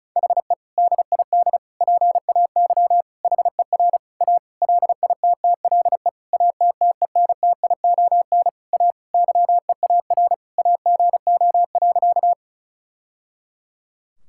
Morse code, heard as a keyed tone, about 36 wpm, 700 Hz. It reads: HE DID PAY HER A LITTLE ATTENTION A YEAR AGO.
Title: Sentences from Top 1000 Words - Review Part 6 - 35wpm